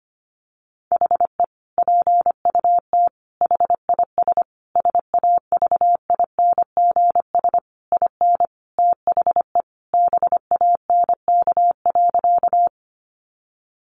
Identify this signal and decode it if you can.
Morse code, heard as a keyed tone, about 25 wpm, 700 Hz. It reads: HE PUT HIS SAVINGS IN THE BANK.
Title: Sentences from Top 1200 Words 25wpm